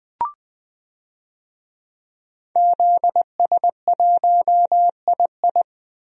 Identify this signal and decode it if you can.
Morse code, heard as a keyed tone, about 21 wpm, 700 Hz. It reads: ZS1II
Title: DXCC-100 Call Signs - 5 Characters - Repeated 3x - 20wpm